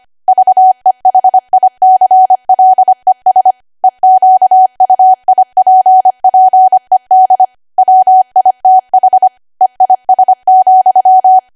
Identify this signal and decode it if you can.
Morse code, heard as a keyed tone, about 25 wpm, 750 Hz. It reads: VEHICLES EQUIPPED WITH EIS,
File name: Archive / 25 WPM / 25WPM